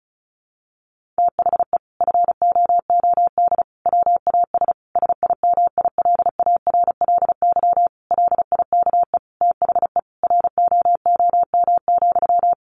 Morse code, heard as a keyed tone, about 34 wpm, 700 Hz. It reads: THE FOOD WAS SIMILARLY LIKE THE ROOM,
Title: Sentences from Top 1400 Words 35wpm